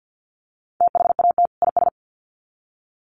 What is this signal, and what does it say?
Morse code, heard as a keyed tone, about 48 wpm, 700 Hz. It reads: THAT IS
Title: Sets of 2 Words 50wpm